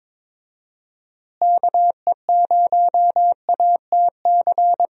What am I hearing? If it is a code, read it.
Morse code, heard as a keyed tone, about 22 wpm, 700 Hz. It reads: KE0ATC
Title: USA Call Signs - 6 Characters - Repeated 3x - 22wpm